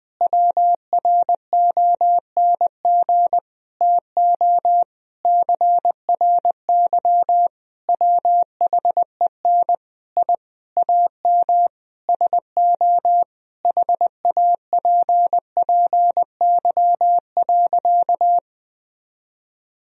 Morse code, heard as a keyed tone, about 20 wpm, 700 Hz. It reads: WRONG TO CRY WHEN I AM SO HAPPY.
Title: Sentences from Top 1000 Words - Review Part 2 - 20wpm